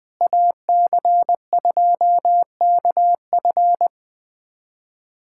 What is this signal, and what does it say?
Morse code, heard as a keyed tone, about 21 wpm, 700 Hz. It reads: AC2KF
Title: USA Callsign DE Callsign - 5 Characters - 20wpm